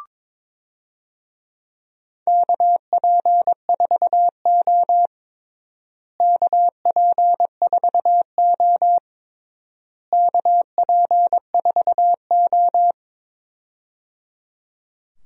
Morse code, heard as a keyed tone, about 22 wpm, 700 Hz. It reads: KP4O KP4O KP4O
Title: DXCC-100 Call Signs - 4 Characters - Repeated 3x - 22wpm